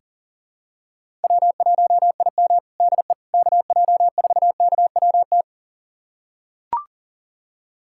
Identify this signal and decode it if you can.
Morse code, heard as a keyed tone, about 37 wpm, 700 Hz. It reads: W1IM DE KJ4KWT
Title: USA Callsign DE Callsign 40wpm